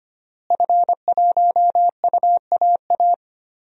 Morse code, heard as a keyed tone, about 25 wpm, 700 Hz. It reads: F1UAA
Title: DXCC-100 Call Signs - 5 Characters - Repeated 3x - 25wpm